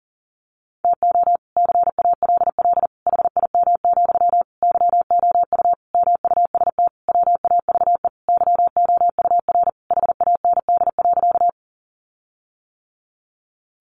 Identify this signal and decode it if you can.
Morse code, heard as a keyed tone, about 40 wpm, 700 Hz. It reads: TO CALL HIM, YOU MUST WAVE YOUR HAND.
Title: Sentences from Top 1400 Words 40wpm